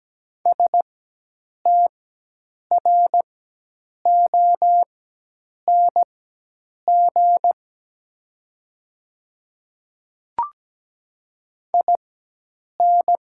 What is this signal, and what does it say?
Morse code, heard as a keyed tone, about 17 wpm, spaced as for 8 wpm, 700 Hz. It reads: STRONG IN
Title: 4x Character Spacing - Top 500 Words - 17wpm